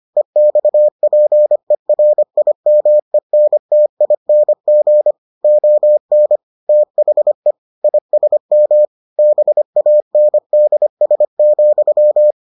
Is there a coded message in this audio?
Yes: Morse code from 0.2 to 12.3 s, heard as a keyed tone, about 25 wpm, 600 Hz. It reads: EXPERIMENTING ON THE ISM BANDS,